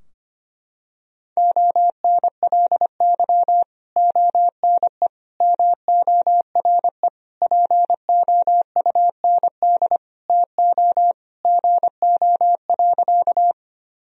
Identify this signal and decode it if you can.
Morse code, heard as a keyed tone, about 25 wpm, 700 Hz. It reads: ONLY ONE MORE POUND TO GO.
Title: Sentences from Top 1000 Words - Review Part 2 - 25wpm